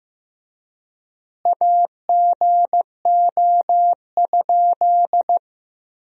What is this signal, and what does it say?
Morse code, heard as a keyed tone, about 15 wpm, 700 Hz. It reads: AGO?
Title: 3x Word-Spacing - Sentences from Top 500 Words - 15wpm